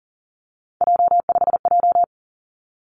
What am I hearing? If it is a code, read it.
Morse code, heard as a keyed tone, about 42 wpm, 700 Hz. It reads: J5J
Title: Three Challenging Letter-Number Combinations 40wpm